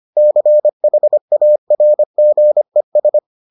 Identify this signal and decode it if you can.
Morse code, heard as a keyed tone, about 25 wpm, 600 Hz. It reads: CHARGES